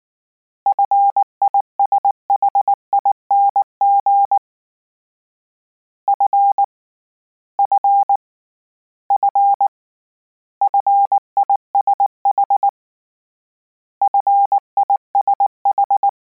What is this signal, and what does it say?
Morse code, heard as a keyed tone, about 19 wpm, 800 Hz. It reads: FISHING F F F FISH FISH